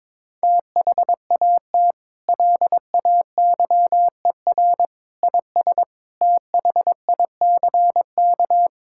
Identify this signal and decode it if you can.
Morse code, heard as a keyed tone, about 21 wpm, 700 Hz. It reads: THAT LAYER IS THICK
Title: Sentences from Top 1600 Words 22wpm